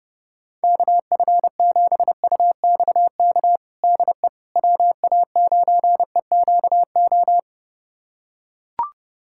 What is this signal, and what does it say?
Morse code, heard as a keyed tone, about 32 wpm, 700 Hz. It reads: KF7UXK DE WA9EQO
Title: USA Callsign DE Callsign 30wpm